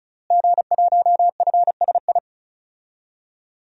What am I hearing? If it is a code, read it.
Morse code, heard as a keyed tone, about 35 wpm, 700 Hz. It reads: G1FSI
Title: DXCC-100 Call Signs - 5 Characters - Repeated 2x - 35wpm